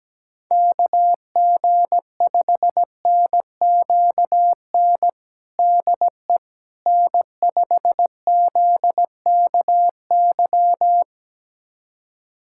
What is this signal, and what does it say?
Morse code, heard as a keyed tone, about 17 wpm, 700 Hz. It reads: KG5NQN DE N5ZKY
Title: USA Callsign DE Callsign 17wpm